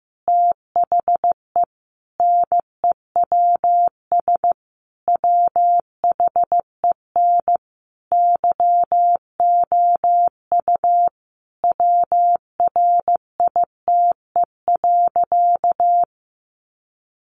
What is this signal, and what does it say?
Morse code, heard as a keyed tone, about 15 wpm, 700 Hz. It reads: THE NEWS WHEN YOU WRITE.
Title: Sentences from Top 800 Words 15wpm